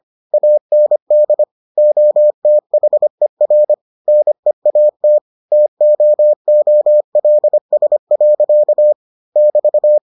Morse code, heard as a keyed tone, about 25 wpm, 600 Hz. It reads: AND OTHER NEAT TOOLS. <BT>